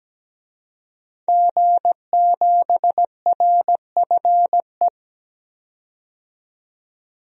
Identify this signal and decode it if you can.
Morse code, heard as a keyed tone, about 17 wpm, 700 Hz. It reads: G7RFE
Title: DXCC-100 Call Signs - 5 Characters - 17wpm